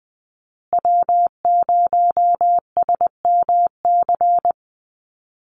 Morse code, heard as a keyed tone, about 20 wpm, 700 Hz. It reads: W0SMC
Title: USA Call Signs Repeated 2x 20wpm